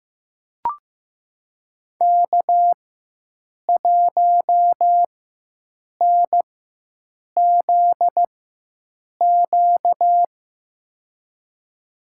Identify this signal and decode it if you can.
Morse code, heard as a keyed tone, about 15 wpm, spaced as for 7 wpm, 700 Hz. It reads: K1NZQ
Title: USA Call Signs - 5 Characters - 4x Character Spacing - 15wpm